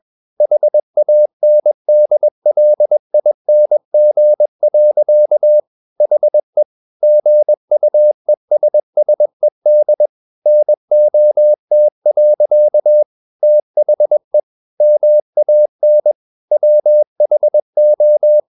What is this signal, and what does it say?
Morse code, heard as a keyed tone, about 21 wpm, 600 Hz. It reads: HANDLING. HE GUESSED NOT. THE MAN WHO
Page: CW med Gnister